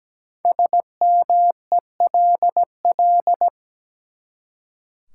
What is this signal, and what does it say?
Morse code, heard as a keyed tone, about 17 wpm, 700 Hz. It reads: SMELL
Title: Five Letter Words - 17wpm